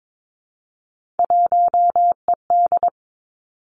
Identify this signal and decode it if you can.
Morse code, heard as a keyed tone, about 22 wpm, 700 Hz. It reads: 1ED
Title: Three Challenging Letter-Number Combinations 22wpm